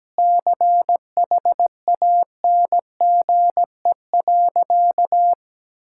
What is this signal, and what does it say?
Morse code, heard as a keyed tone, about 17 wpm, 700 Hz. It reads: CHANGE.
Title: Sentences from Top 1000 Words - Review Part 5 - 17wpm